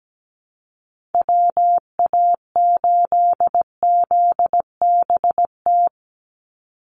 Morse code, heard as a keyed tone, about 17 wpm, 700 Hz. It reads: WA8ZBT
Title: Call Signs Repeated 3x - 17wpm